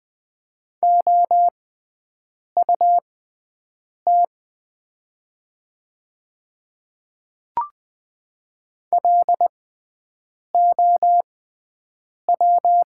Morse code, heard as a keyed tone, about 20 wpm, spaced as for 7 wpm, 700 Hz. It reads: OUT LOW
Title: 6x Character Spacing - Top 200 Words - 20wpm